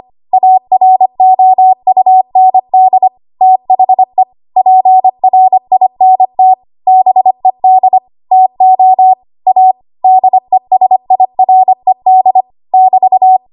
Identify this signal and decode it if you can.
Morse code, heard as a keyed tone, about 25 wpm, 750 Hz. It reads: AROUND THE PRINT BED TO A DESIRED <BT>
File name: Archive / 25 WPM / 25WPM